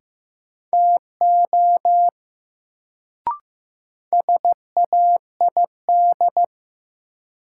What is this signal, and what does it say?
Morse code, heard as a keyed tone, about 15 wpm, 700 Hz. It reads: TO SAID
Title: Sets of 2 Words - New Word Drill - 15wpm